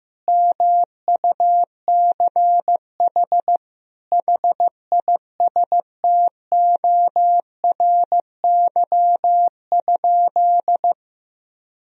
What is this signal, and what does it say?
Morse code, heard as a keyed tone, about 15 wpm, 700 Hz. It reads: MUCH HISTORY?
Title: Sentences from Top 1000 Words - Review Part 4 - 15wpm